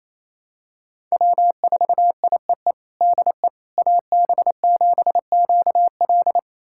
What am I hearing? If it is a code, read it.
Morse code, heard as a keyed tone, about 28 wpm, 700 Hz. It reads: W4IEE DE AB7QL
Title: USA Callsign DE Callsign - 5 Characters - 28wpm